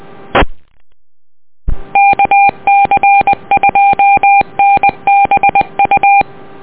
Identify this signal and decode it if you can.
Morse code, heard as a keyed tone, about 20 wpm, 800 Hz. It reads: E KC2NBU
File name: net audio